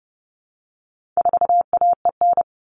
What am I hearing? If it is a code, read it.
Morse code, heard as a keyed tone, about 30 wpm, 700 Hz. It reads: 4AEN
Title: All Cut Numbers - 4-Digit 30wpm